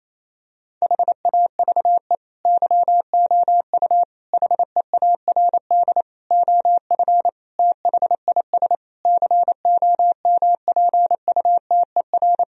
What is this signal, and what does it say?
Morse code, heard as a keyed tone, about 28 wpm, 700 Hz. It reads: HAVE YOU HEARD OF THIS COMPUTER
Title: Sentences from Top 1000 Words - Review Part 3 - 28wpm